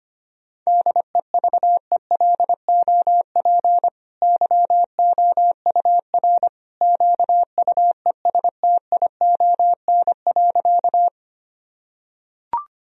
Morse code, heard as a keyed tone, about 25 wpm, 700 Hz. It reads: DEVELOP YOUR QUESTION.
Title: Sentences from Top 1000 Words - Review Part 2 - 25wpm